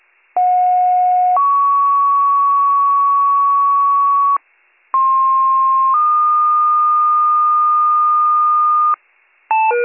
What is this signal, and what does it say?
Two-tone page, heard as a one-second tone, then a three-second tone, about 730 then 1120 Hz, about 1060 then 1220 Hz. It is heard from 0.4 to 9.0 s.